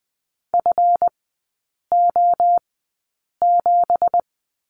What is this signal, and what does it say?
Morse code, heard as a keyed tone, about 20 wpm, spaced as for 8 wpm, 700 Hz. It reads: FO7
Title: 184 Lesson 30e - Letter b - Three Characters 20wpm